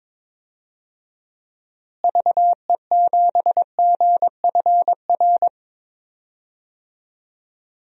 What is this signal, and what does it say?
Morse code, heard as a keyed tone, about 22 wpm, 700 Hz. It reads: VE7GFR
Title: DXCC-100 Call Signs - 6 Characters - Repeated 3x - 22wpm